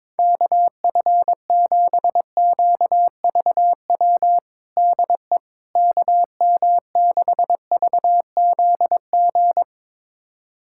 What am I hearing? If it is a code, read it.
Morse code, heard as a keyed tone, about 22 wpm, 700 Hz. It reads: KF7QVW DE KM6VZG
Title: USA Callsign DE Callsign 22wpm